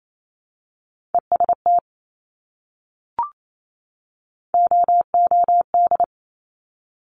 Morse code, heard as a keyed tone, about 28 wpm, 690 Hz. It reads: EST OOD
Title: Top Three Letter Combinations in Words 28wpm